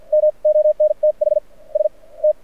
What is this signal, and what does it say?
Morse code, heard as a keyed tone, about 57 wpm, spaced as for 35 wpm, 590 Hz. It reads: MONTHS T
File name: archived / CRSnet